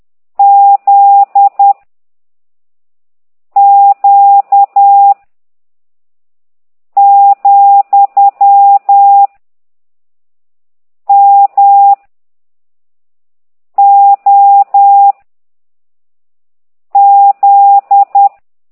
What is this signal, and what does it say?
Morse code, heard as a keyed tone, about 10 wpm, 800 Hz. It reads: Z Q , M O Z